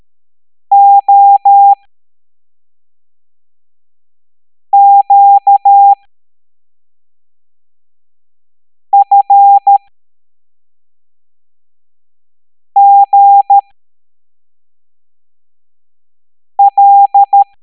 Morse code, heard as a keyed tone, about 13 wpm, spaced as for 3 wpm, 800 Hz. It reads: OQFGL
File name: hamdata / L09-EX04